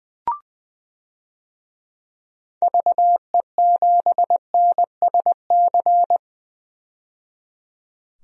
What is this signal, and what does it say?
Morse code, heard as a keyed tone, about 20 wpm, 700 Hz. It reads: VE7NSC
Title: DXCC-100 Call Signs - 6 Characters - 20wpm